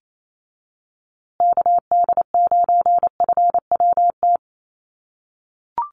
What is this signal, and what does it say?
Morse code, heard as a keyed tone, about 28 wpm, 700 Hz. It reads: KD9FWT
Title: USA Call Signs Repeated 2x 28wpm